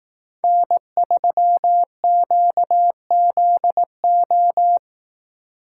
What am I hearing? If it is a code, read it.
Morse code, heard as a keyed tone, about 16 wpm, 700 Hz. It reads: N3QZO
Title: Speed Racing - USA 1x3 Call Signs 15wpm